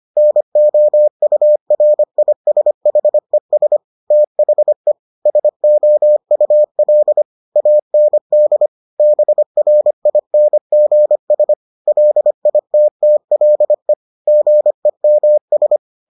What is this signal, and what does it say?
Morse code, heard as a keyed tone, about 25 wpm, 600 Hz. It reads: NOURISHES THE SOUL AND BRINGS LITTLE GEMS